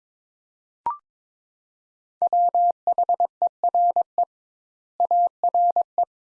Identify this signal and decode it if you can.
Morse code, heard as a keyed tone, about 22 wpm, 700 Hz. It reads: WHERE ARE
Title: Sentences Top 200 Words - 22wpm